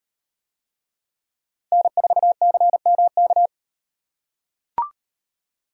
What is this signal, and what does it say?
Morse code, heard as a keyed tone, about 36 wpm, 700 Hz. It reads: N4CMK